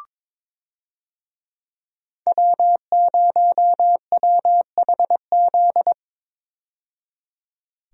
Morse code, heard as a keyed tone, about 22 wpm, 700 Hz. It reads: W0WHZ
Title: USA Call Signs - 5 Characters - 22wpm